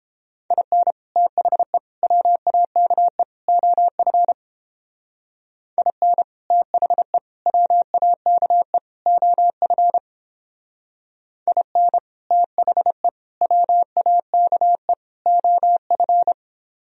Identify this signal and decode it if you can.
Morse code, heard as a keyed tone, about 29 wpm, 700 Hz. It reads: IN THE WAKE OF IN THE WAKE OF IN THE WAKE OF